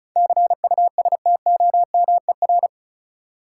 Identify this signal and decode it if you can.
Morse code, heard as a keyed tone, about 36 wpm, 700 Hz. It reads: CUSTOMER
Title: Sentences from Top 1100 Words - New Word Drill - 35wpm